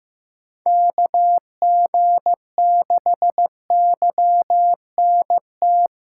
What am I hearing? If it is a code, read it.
Morse code, heard as a keyed tone, about 15 wpm, 700 Hz. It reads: KG6YNT
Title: USA Callsign DE Callsign - 6 Characters - 15wpm